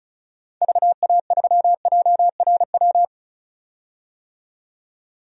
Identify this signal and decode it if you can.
Morse code, heard as a keyed tone, about 34 wpm, 700 Hz. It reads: VA3JRW
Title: DXCC-100 Call Signs - 6 Characters - 35wpm